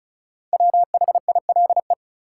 Morse code, heard as a keyed tone, about 36 wpm, 700 Hz. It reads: WHILE
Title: Sentences from Top 300 Words - New Word Drill - 35wpm